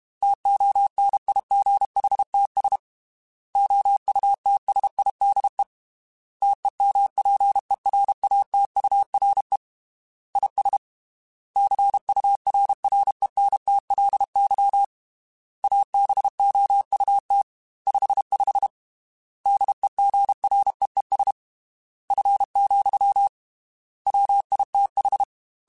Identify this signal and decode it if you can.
Morse code, heard as a keyed tone, about 32 wpm, 800 Hz. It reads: TONIGHTS OUTSIDE TEMPERATURE IS CURRENTLY ABOUT 55 DEGREES F, WITH
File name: Weather Sets / Weather Sets-mp3s / Weather 4 24 @30